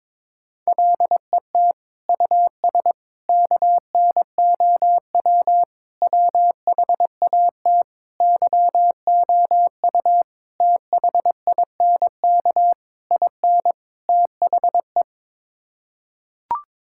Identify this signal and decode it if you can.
Morse code, heard as a keyed tone, about 21 wpm, 700 Hz. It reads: LET US KNOW WHAT YOU THINK IN THE
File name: practice / 22wpm / Sets of 8 Words - 22wpm